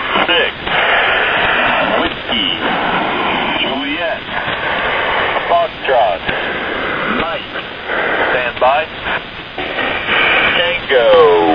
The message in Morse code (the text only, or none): none